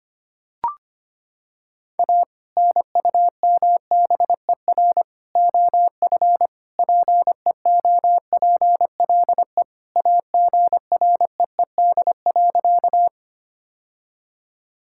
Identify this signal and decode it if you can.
Morse code, heard as a keyed tone, about 25 wpm, 700 Hz. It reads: A NUMBER OF PEOPLE AGREED.
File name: practice / 25wpm / Sentences with 3-Word Phrases - 25wpm